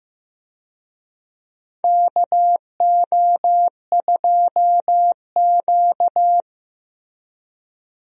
Morse code, heard as a keyed tone, about 15 wpm, 700 Hz. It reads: KO2Q
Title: USA Call Signs - 4 Characters - Repeated 2x - 15wpm